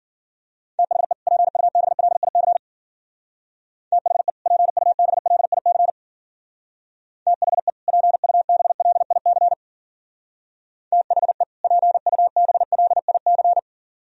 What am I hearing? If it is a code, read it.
Morse code, heard as a keyed tone, about 50 wpm, 700 Hz. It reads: THE PUBLIC THE PUBLIC THE PUBLIC THE PUBLIC